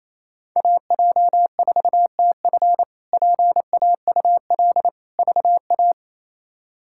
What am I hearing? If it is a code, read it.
Morse code, heard as a keyed tone, about 28 wpm, 700 Hz. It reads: AJ4TF PAUL VA
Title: K1USN SST QSO Practice - 28wpm